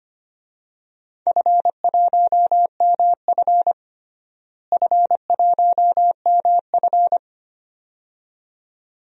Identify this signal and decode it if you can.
Morse code, heard as a keyed tone, about 25 wpm, 700 Hz. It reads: F1MF F1MF
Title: DXCC-100 Call Signs - 4 Characters - Repeated 2x - 25wpm